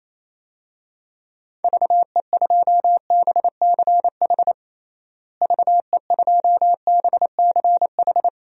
Morse code, heard as a keyed tone, about 28 wpm, 700 Hz. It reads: VE2BCH VE2BCH
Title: DXCC-100 Call Signs - 6 Characters - Repeated 2x - 28wpm